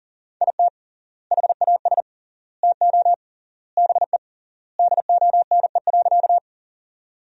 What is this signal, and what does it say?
Morse code, heard as a keyed tone, about 40 wpm, 700 Hz. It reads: IT HAS TO BE DONE.
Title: 3x Word-Spacing - Sentences from Top 500 Words - 40wpm